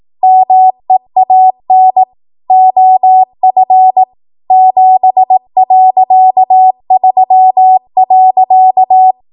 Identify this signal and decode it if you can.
Morse code, heard as a keyed tone, about 18 wpm, 750 Hz. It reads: MEAN OF 7.3.
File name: Archive / 18 WPM / 18WPM